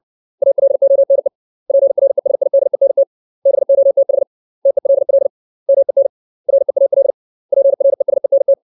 Morse code, heard as a keyed tone, about 60 wpm, 550 Hz. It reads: ALONE PRESIDENT BOTH TELL MEN REAL PRESENT
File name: 632 words / 632 Words 60WPM